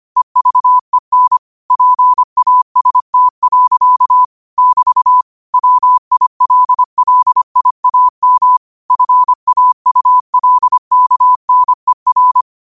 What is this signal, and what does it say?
Morse code, heard as a keyed tone, about 25 wpm, 1000 Hz. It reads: EVEN PAST. <BT> WILLIAM FAULKNER